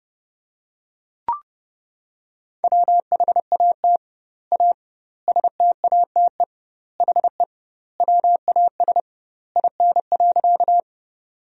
Morse code, heard as a keyed tone, about 32 wpm, 700 Hz. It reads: WHAT A STATE HE WAS IN.
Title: 2x Word-Spacing - Sentences from Top 400 Words - 30wpm